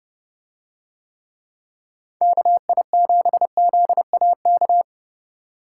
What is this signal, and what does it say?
Morse code, heard as a keyed tone, about 29 wpm, 700 Hz. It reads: KI7ZAK